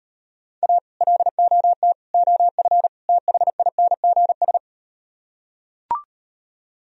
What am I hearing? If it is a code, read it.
Morse code, heard as a keyed tone, about 38 wpm, 700 Hz. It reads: A LOT OF THINGS